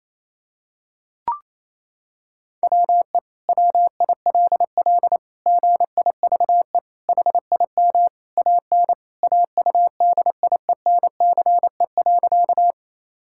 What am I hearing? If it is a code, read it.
Morse code, heard as a keyed tone, about 28 wpm, 700 Hz. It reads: WE WILL GIVE HIM AN AUDIENCE.
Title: Sentences from Top 1000 Words - Review Part 5 - 28wpm